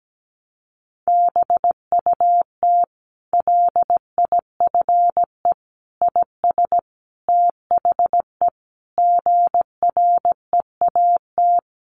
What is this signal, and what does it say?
Morse code, heard as a keyed tone, about 17 wpm, 700 Hz. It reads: BUT LIFE IS THE GREAT
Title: Sentences from Top 900 Words 17wpm